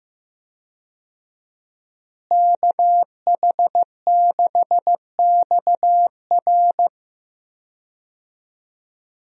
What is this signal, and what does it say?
Morse code, heard as a keyed tone, about 15 wpm, 700 Hz. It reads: KH6XR
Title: DXCC-100 Call Signs - 5 Characters - Repeated 3x - 15wpm